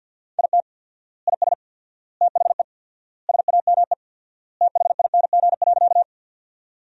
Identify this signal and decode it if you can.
Morse code, heard as a keyed tone, about 52 wpm, 700 Hz. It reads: IT IS THE SAME THING.
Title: 4x Word-Spacing - Sentences from Top 200 Words - 50wpm